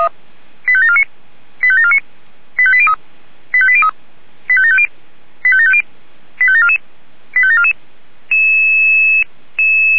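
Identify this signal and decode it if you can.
Five-tone selective call (ZVEI), heard as quick runs of five tones, tones of about 70 ms. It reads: 85749 85749 85803 85803 85750 85750 85744 85744
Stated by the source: ZVEI five-tone selective call received by SDR.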